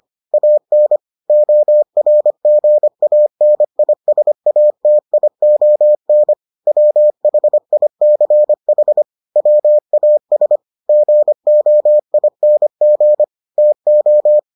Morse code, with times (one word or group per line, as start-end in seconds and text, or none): 0.3-1.0 AN
1.3-6.3 ORGANISATION
6.7-9.0 WHICH
9.4-10.6 WAS
10.9-13.2 GOING
13.6-14.4 TO